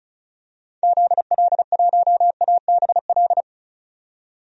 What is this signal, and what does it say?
Morse code, heard as a keyed tone, about 36 wpm, 700 Hz. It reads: ZL1ABL
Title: DXCC-100 Call Signs - 6 Characters - 35wpm